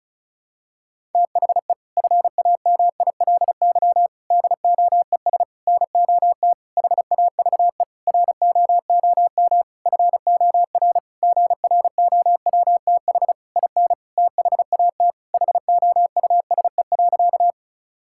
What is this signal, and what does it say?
Morse code, heard as a keyed tone, about 36 wpm, 700 Hz. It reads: THE FAMILY DOES NOT HAVE ROOM FOR GROWTH IN THAT HOUSE.
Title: Sentences from Top 1000 Words - Review Part 5 - 35wpm